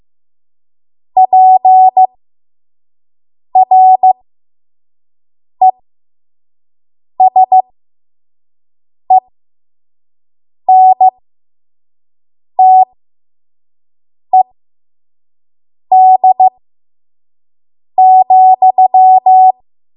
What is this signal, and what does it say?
Morse code, heard as a keyed tone, about 15 wpm, spaced as for 5 wpm, 750 Hz. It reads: PRESENTED,
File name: Morse / 05WPM